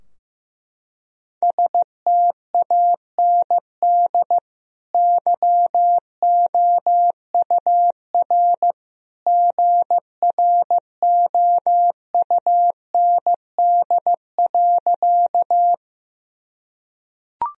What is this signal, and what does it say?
Morse code, heard as a keyed tone, about 15 wpm, 700 Hz. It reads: STAND YOUR GROUND.
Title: Sentences from Top 1000 Words - Review Part 2 - 15wpm